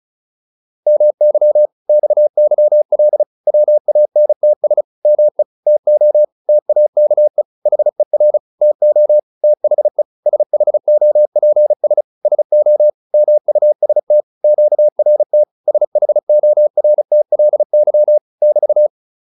Morse code, heard as a keyed tone, about 34 wpm, 600 Hz. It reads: MY XYL WANTS ME TO TAKE HER TO THE SHOPS SO MUST QRT SHORTLY <BT>
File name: QSO406 35